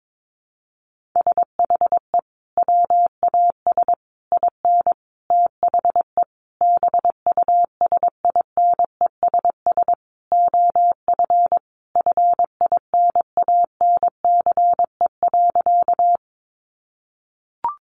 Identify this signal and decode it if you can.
Morse code, heard as a keyed tone, about 22 wpm, 700 Hz. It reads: SHE WAS IN THE BUSINESS OF FINANCE.
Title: Sentences from Top 1500 Words 22wpm